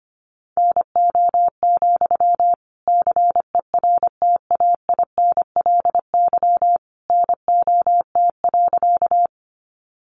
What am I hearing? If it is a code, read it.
Morse code, heard as a keyed tone, about 25 wpm, 700 Hz. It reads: NO, CERTAINLY NOT.
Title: Sentences from Top 900 Words 25wpm